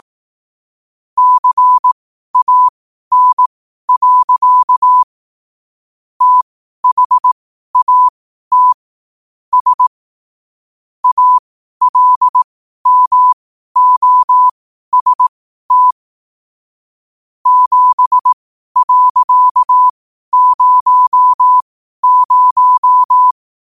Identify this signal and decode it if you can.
Morse code, heard as a keyed tone, about 18 wpm, spaced as for 12 wpm, 1000 Hz. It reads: CAN. THAT S ALMOST 7.00
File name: qotdfiles / 12wpm